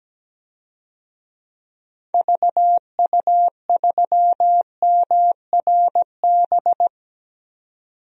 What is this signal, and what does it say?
Morse code, heard as a keyed tone, about 17 wpm, 700 Hz. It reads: VU3MRB
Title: DXCC-100 Call Signs - 6 Characters - 17wpm